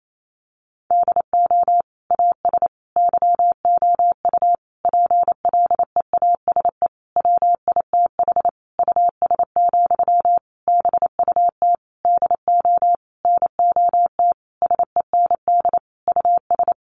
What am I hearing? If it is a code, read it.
Morse code, heard as a keyed tone, about 28 wpm, 700 Hz. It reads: DO AS YOU PLEASE WITH US, BUT DO NOT SEND US
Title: Sentences from Top 1500 Words 28wpm